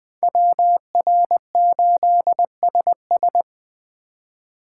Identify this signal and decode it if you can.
Morse code, heard as a keyed tone, about 24 wpm, 700 Hz. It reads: WR8SS
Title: Speed Racing - USA 2x2 Call Signs 20wpm